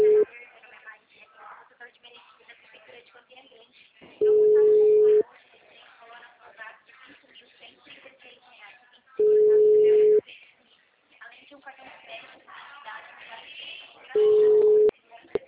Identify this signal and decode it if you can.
Ringback tone (Europe, 425 Hz), with speech.